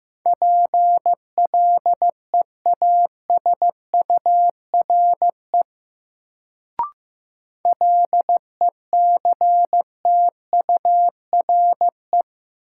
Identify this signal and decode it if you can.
Morse code, heard as a keyed tone, about 15 wpm, 700 Hz. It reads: PLEASURE LECTURE
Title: Sentences from Top 1600 Words - New Word Drill - 15wpm